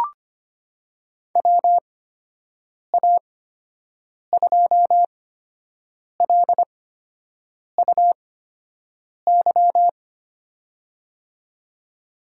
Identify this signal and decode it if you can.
Morse code, heard as a keyed tone, about 25 wpm, spaced as for 7 wpm, 700 Hz. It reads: WA2LUY
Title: USA Call Signs - 6 Characters - 8x Character Spacing - 25wpm